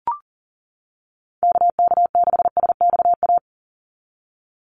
Morse code, heard as a keyed tone, about 40 wpm, 700 Hz. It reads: KK6SXA
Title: USA Call Signs 40wpm